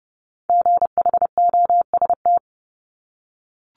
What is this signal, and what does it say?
Morse code, heard as a keyed tone, about 30 wpm, 700 Hz. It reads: GHOST
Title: Multi-letter Phonemes Repeated 3x 30wpm